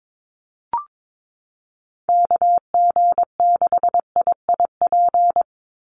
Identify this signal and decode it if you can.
Morse code, heard as a keyed tone, about 22 wpm, 700 Hz. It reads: KG6IIP
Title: Call Signs - 22wpm